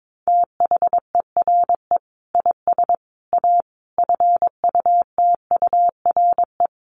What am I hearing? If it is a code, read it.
Morse code, heard as a keyed tone, about 22 wpm, 700 Hz. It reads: THERE IS A FUTURE
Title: Sentences from Top 700 Words 22wpm